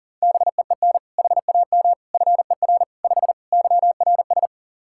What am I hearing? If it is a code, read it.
Morse code, heard as a keyed tone, about 40 wpm, 700 Hz. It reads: BEEN HAM FER 5 YRS
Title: K1USN SST QSO Practice 40wpm